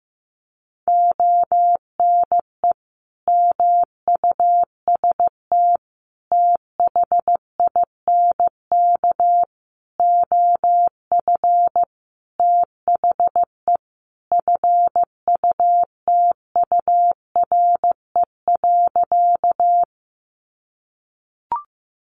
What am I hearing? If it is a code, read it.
Morse code, heard as a keyed tone, about 15 wpm, 700 Hz. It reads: ONE MUST THINK OF THE FUTURE.
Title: Sentences from Top 700 Words 15wpm